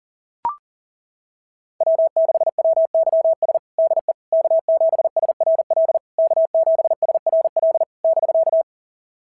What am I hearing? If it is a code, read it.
Morse code, heard as a keyed tone, about 39 wpm, 690 Hz. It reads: W6WYS DE K7SRL K7SRL <BK>
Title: SOTA QSO Practice 40wpm